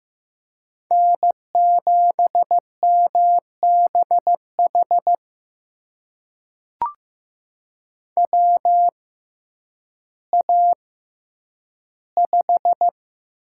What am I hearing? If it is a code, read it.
Morse code, heard as a keyed tone, about 15 wpm, spaced as for 9 wpm, 700 Hz. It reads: N7MBH WA5
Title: USA Call Signs - 6x Character Spacing - 15wpm